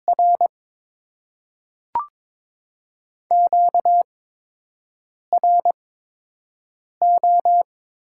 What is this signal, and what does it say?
Morse code, heard as a keyed tone, about 22 wpm, spaced as for 6 wpm, 700 Hz. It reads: R QRO
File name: Top 90 QSO Elements - 8x Character Spacing - 22wpm